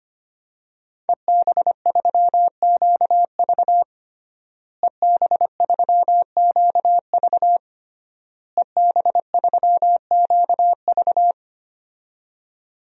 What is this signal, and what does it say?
Morse code, heard as a keyed tone, about 25 wpm, 700 Hz. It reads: EB3QV EB3QV EB3QV
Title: DXCC-100 Call Signs - 5 Characters - Repeated 3x - 25wpm